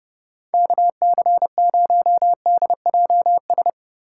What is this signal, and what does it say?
Morse code, heard as a keyed tone, about 30 wpm, 700 Hz. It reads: KC0DJS
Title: USA Call Signs - 6 Characters - 2x Character Spacing - 30wpm